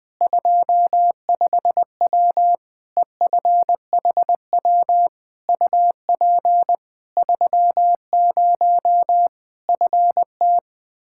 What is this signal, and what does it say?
Morse code, heard as a keyed tone, about 20 wpm, 700 Hz. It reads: 25W EFHW UP 30 FT
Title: Practice QSO - Exchange 2 - Info - 20wpm